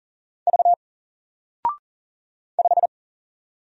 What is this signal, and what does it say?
Morse code, heard as a keyed tone, about 40 wpm, 700 Hz. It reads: V 5
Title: Single Letter-Number 40wpm